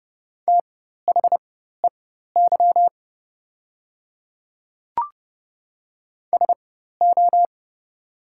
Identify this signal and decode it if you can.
Morse code, heard as a keyed tone, about 30 wpm, spaced as for 14 wpm, 700 Hz. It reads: THEY SO
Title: 4x Character Spacing - Top 200 Words - 30wpm